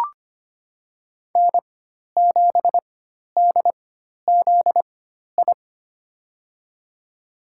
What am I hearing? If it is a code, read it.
Morse code, heard as a keyed tone, about 25 wpm, spaced as for 12 wpm, 700 Hz. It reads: N7DZI
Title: USA Call Signs - 5 Characters - 4x Character Spacing - 25wpm